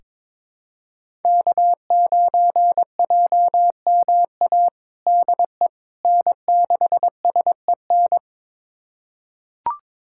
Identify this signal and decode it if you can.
Morse code, heard as a keyed tone, about 22 wpm, 700 Hz. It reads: K9JMA DE N6SEN
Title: USA Callsign DE Callsign - 5 Characters - 22wpm